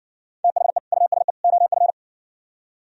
Morse code, heard as a keyed tone, about 69 wpm, 700 Hz. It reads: THE USE OF